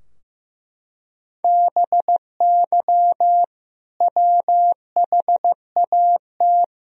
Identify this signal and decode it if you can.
Morse code, heard as a keyed tone, about 15 wpm, 700 Hz. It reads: BY WHAT
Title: Sentences from Top 1000 Words - Review Part 3 - 15wpm